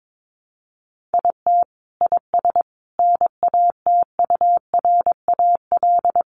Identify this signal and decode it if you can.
Morse code, heard as a keyed tone, about 22 wpm, 700 Hz. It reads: IT IS NATURAL
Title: Sentences from Top 700 Words 22wpm